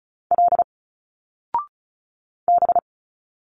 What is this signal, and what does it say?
Morse code, heard as a keyed tone, about 34 wpm, 700 Hz. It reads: L B
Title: Single Letters 35wpm